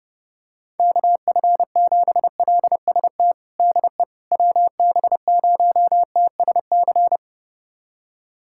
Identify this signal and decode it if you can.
Morse code, heard as a keyed tone, about 30 wpm, 700 Hz. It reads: KF7LST DE WB0TSC
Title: USA Callsign DE Callsign 30wpm